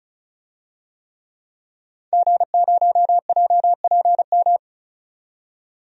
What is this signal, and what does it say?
Morse code, heard as a keyed tone, about 36 wpm, 700 Hz. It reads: G0JPM
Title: DXCC-100 Call Signs - 5 Characters - 35wpm